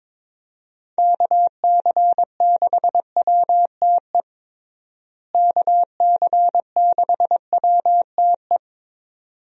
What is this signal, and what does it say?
Morse code, heard as a keyed tone, about 22 wpm, 700 Hz. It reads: KC6WTE KC6WTE
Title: USA Call Signs - 6 Characters - Repeated 2x - 22wpm